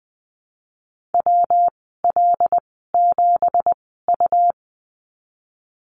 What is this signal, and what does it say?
Morse code, heard as a keyed tone, about 20 wpm, spaced as for 15 wpm, 700 Hz. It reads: WL7U
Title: 193 Lesson 31g - Letter v - Call Signs 20wpm